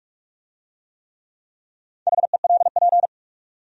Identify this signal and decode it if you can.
Morse code, heard as a keyed tone, about 40 wpm, 700 Hz. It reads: HELP